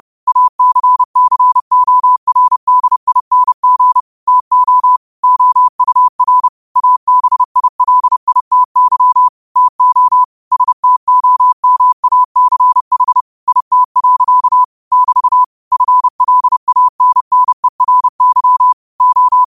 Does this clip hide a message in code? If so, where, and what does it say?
Yes: Morse code from 0.3 to 19.4 s, heard as a keyed tone, about 30 wpm, 1000 Hz. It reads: ACCORDING TO OUR ABILITY TO STOMACH IT. <BT> FLANNERY O